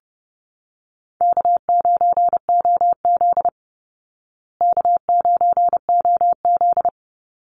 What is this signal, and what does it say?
Morse code, heard as a keyed tone, about 30 wpm, 700 Hz. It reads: K9OZ K9OZ
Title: Call Signs Repeated 3x - 30wpm